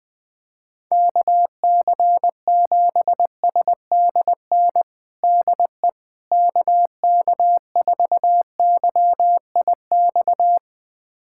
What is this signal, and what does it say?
Morse code, heard as a keyed tone, about 20 wpm, 700 Hz. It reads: KC7SDN DE KK4YIX
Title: USA Callsign DE Callsign - 6 Characters - 20wpm